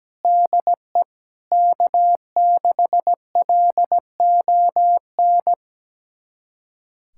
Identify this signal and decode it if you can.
Morse code, heard as a keyed tone, about 17 wpm, 700 Hz. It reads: DE K6LON
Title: USA Callsign DE Callsign - 5 Characters - 17wpm